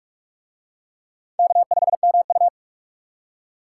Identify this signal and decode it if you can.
Morse code, heard as a keyed tone, about 44 wpm, 700 Hz. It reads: K5MU